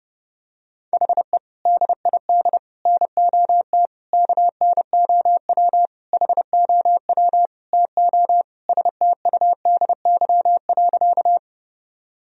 Morse code, heard as a keyed tone, about 31 wpm, 700 Hz. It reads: HE DID NOT KNOW HOW TO STUDY.
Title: Sentences from Top 1000 Words - Review Part 1 - 30wpm